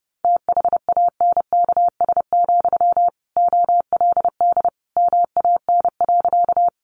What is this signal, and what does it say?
Morse code, heard as a keyed tone, about 30 wpm, 700 Hz. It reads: THANKS, OLD MAN.
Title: Sentences from Top 600 Words 30wpm